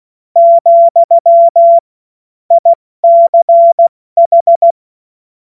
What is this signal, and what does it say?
Morse code, heard as a keyed tone, about 16 wpm, spaced as for 14 wpm, 680 Hz. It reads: , ICH
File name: Morsekurs / Witze / Witz 24. August 2021, 70 F80